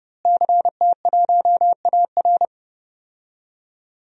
Morse code, heard as a keyed tone, about 30 wpm, 700 Hz. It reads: CT1AR
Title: DXCC-100 Call Signs - 5 Characters - Repeated 3x - 30wpm